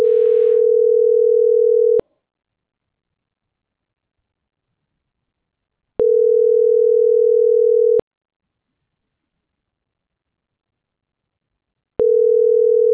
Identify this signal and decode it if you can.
Ringback tone (North America), 440 + 480 Hz.